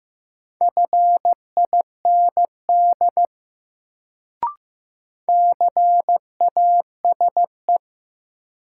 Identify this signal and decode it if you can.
Morse code, heard as a keyed tone, about 15 wpm, 700 Hz. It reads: FIND CASE
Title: Sets of 2 Words - New Word Drill - 15wpm